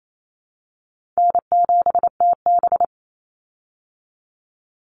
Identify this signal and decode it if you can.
Morse code, heard as a keyed tone, about 28 wpm, 700 Hz. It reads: N7TB
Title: Call Signs Repeated 2x - 28wpm